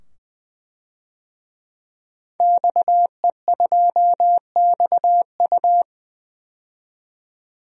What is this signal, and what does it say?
Morse code, heard as a keyed tone, about 20 wpm, 700 Hz. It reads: XE2XU
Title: DXCC-100 Call Signs - 5 Characters - 20wpm